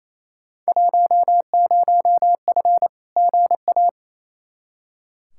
Morse code, heard as a keyed tone, about 28 wpm, 700 Hz. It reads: 10F GA
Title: Field Day Exchanges Encore - 28wpm